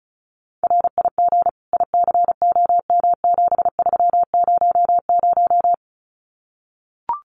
Morse code, heard as a keyed tone, about 34 wpm, 700 Hz. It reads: RIG ICOM7300
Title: Common QSO Phrases Encore 35wpm